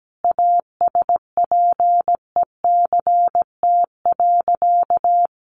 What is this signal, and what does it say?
Morse code, heard as a keyed tone, about 17 wpm, 700 Hz. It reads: ASPECT.
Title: Sentences from Top 1300 Words 17wpm